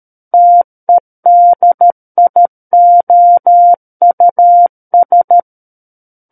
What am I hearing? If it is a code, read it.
Morse code, heard as a keyed tone, about 13 wpm, 700 Hz. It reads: TEDIOUS
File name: Book One / 04-Chapter 4 - 13 WPM